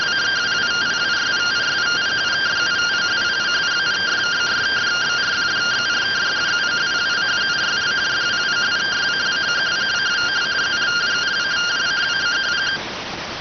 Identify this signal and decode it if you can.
Two-tone teleprinter signal, about 170 Hz shift.